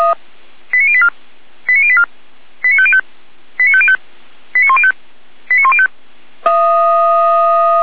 Five-tone selective call (ZVEI), heard as quick runs of five tones, tones of about 70 ms. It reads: 89084 89084 89505 89505 89105 89105